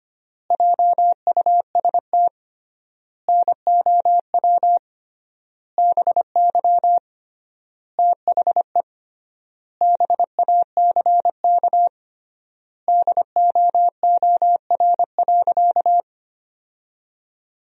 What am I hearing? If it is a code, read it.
Morse code, heard as a keyed tone, about 25 wpm, 700 Hz. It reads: JUST NOW BY THE BACK DOOR.
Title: 3x Word-Spacing - Sentences from Top 300 Words - 25wpm